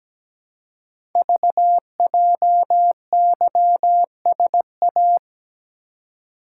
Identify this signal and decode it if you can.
Morse code, heard as a keyed tone, about 17 wpm, 700 Hz. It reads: VJYSA
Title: Code Groups - 5 Letters - 17wpm